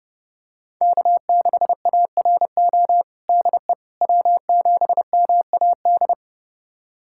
Morse code, heard as a keyed tone, about 32 wpm, 700 Hz. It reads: K6ARO DE W7MAD
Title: USA Callsign DE Callsign - 5 Characters - 30wpm